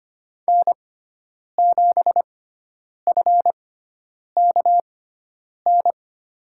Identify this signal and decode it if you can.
Morse code, heard as a keyed tone, about 25 wpm, spaced as for 9 wpm, 700 Hz. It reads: N7FKN
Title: USA Call Signs - 5 Characters - 6x Character Spacing - 25wpm